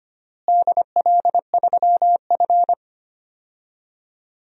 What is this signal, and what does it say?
Morse code, heard as a keyed tone, about 25 wpm, 700 Hz. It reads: DL3F
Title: DXCC-100 Call Signs - 4 Characters - Repeated 3x - 25wpm